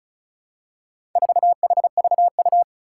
Morse code, heard as a keyed tone, about 36 wpm, 690 Hz. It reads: 4HVU
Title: Four Challenging Letter-Number Combinations 35wpm